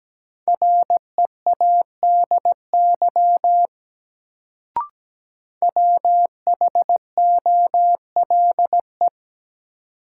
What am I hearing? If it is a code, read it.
Morse code, heard as a keyed tone, about 17 wpm, 700 Hz. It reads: READY WHOLE
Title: Sentences from Top 400 Words - New Word Drill - 17wpm